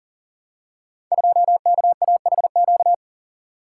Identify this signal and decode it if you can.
Morse code, heard as a keyed tone, about 38 wpm, 700 Hz. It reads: 2KAHQ
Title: Code Groups - 5 Letter:Numbers - 40wpm